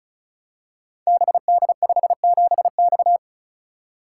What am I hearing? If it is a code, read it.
Morse code, heard as a keyed tone, about 36 wpm, 700 Hz. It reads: BD57X
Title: Code Groups - 5 Letter:Numbers - 35wpm